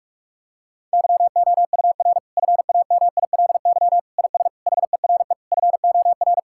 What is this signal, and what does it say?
Morse code, heard as a keyed tone, about 46 wpm, 700 Hz. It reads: YOUR FAMILY IS HERE FOR
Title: Sentences from Top 1600 Words 45wpm